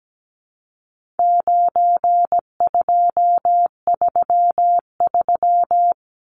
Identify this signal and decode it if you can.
Morse code, heard as a keyed tone, about 17 wpm, 700 Hz. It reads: 9233
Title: Numbers - 4-Digit 17wpm